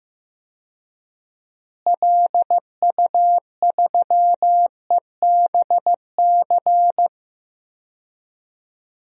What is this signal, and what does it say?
Morse code, heard as a keyed tone, about 15 wpm, 700 Hz. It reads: LU3EBC
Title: DXCC-100 Call Signs - 6 Characters - Repeated 3x - 15wpm